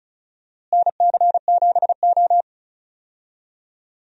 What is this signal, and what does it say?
Morse code, heard as a keyed tone, about 34 wpm, 700 Hz. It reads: NC7O
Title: USA Call Signs - 4 Characters - Repeated 3x - 35wpm